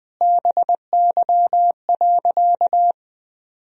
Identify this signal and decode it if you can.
Morse code, heard as a keyed tone, about 20 wpm, 700 Hz. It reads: BY.
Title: Sentences from Top 1000 Words - Review Part 1 - 20wpm